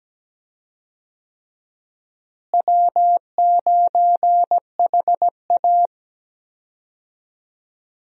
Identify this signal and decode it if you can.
Morse code, heard as a keyed tone, about 17 wpm, 700 Hz. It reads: W9HA
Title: USA Call Signs - 4 Characters - Repeated 3x - 17wpm